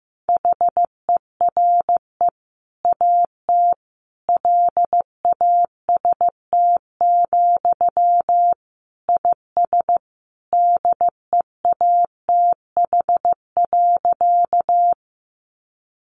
Morse code, heard as a keyed tone, about 15 wpm, 700 Hz. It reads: HERE AT LAST, IS DEATH.
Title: Sentences from Top 1100 Words 15wpm